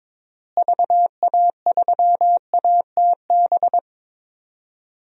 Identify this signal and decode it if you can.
Morse code, heard as a keyed tone, about 22 wpm, 700 Hz. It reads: VA3ATB
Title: DXCC-100 Call Signs - 6 Characters - Repeated 3x - 22wpm